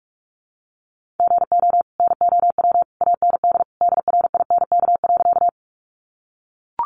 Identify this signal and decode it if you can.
Morse code, heard as a keyed tone, about 44 wpm, 700 Hz. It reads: GO NOW AND DRINK.
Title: Sentences from Top 1500 Words 45wpm